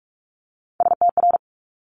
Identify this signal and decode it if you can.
Morse code, heard as a keyed tone, about 46 wpm, 700 Hz. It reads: STR
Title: Multi-letter Phonemes Quick Fire 45wpm